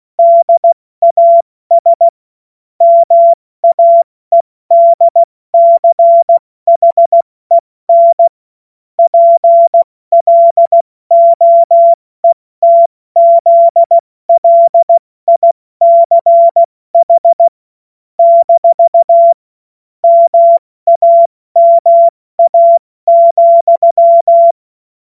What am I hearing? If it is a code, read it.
Morse code, heard as a keyed tone, about 16 wpm, 680 Hz. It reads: DAS MAEDCHEN PLOETZLICH - MAMA,
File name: Morsekurs / Witze / Witz 23. August 2022 70 F80